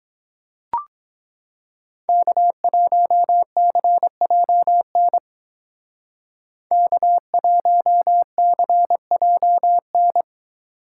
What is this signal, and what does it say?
Morse code, heard as a keyed tone, about 22 wpm, 700 Hz. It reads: K1CJN K1CJN